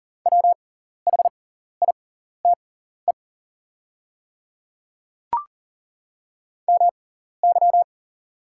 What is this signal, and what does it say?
Morse code, heard as a keyed tone, about 38 wpm, spaced as for 14 wpm, 700 Hz. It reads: WHITE MY
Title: 6x Character Spacing - Top 500 Words - 40wpm